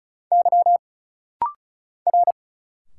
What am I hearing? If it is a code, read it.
Morse code, heard as a keyed tone, about 36 wpm, 700 Hz. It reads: Y R
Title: Single Letter-Number Alt - Rapid-Fire 35wpm